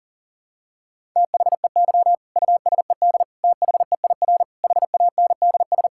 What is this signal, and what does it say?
Morse code, heard as a keyed tone, about 37 wpm, 700 Hz. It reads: THEY USED THEIR HANDS
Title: Sentences from Top 1000 Words - Review Part 6 - 40wpm